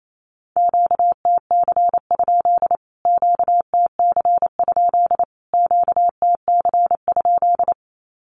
Morse code, heard as a keyed tone, about 28 wpm, 700 Hz. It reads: QTC? QTC? QTC?
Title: Q-Codes Repeated 3x 28wpm